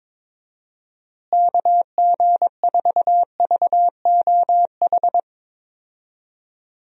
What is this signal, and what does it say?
Morse code, heard as a keyed tone, about 22 wpm, 700 Hz. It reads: KG4VOH
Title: USA Call Signs - 6 Characters - Repeated 3x - 22wpm